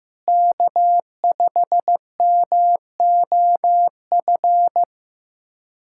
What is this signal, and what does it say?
Morse code, heard as a keyed tone, about 15 wpm, 700 Hz. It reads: K5MOF
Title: USA Callsign DE Callsign 15wpm